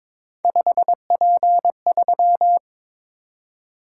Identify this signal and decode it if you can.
Morse code, heard as a keyed tone, about 22 wpm, 700 Hz. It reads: 5P3
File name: practice / 22wpm / Three Letter-Number Combinations - 22wpm